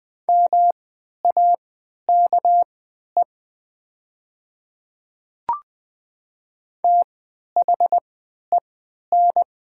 Morse code, heard as a keyed tone, about 20 wpm, spaced as for 11 wpm, 700 Hz. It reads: MAKE THEN
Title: 3x Character Spacing - Top 100 Words - 20wpm